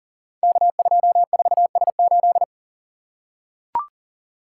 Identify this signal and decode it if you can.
Morse code, heard as a keyed tone, about 42 wpm, 700 Hz. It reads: K24S8
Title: Code Groups - 5 Letter:Numbers - 40wpm